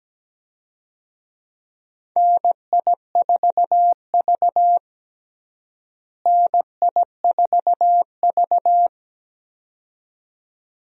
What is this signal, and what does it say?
Morse code, heard as a keyed tone, about 17 wpm, 700 Hz. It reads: NI4V NI4V
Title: USA Call Signs - 4 Characters - Repeated 2x - 17wpm